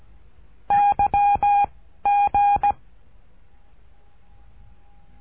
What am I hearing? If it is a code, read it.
Morse code, heard as a keyed tone, about 17 wpm, 810 Hz. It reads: Y G
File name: cursos / CURSO DE TELEGRAFIA / Telegrafia - Trilha 13